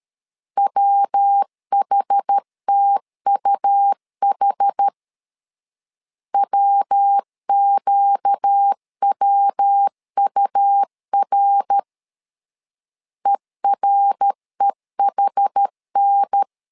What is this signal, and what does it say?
Morse code, heard as a keyed tone, about 13 wpm, 800 Hz. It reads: WHTUH WQWUR EREHN
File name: LEZIONI / 4 Q 12wpm 1833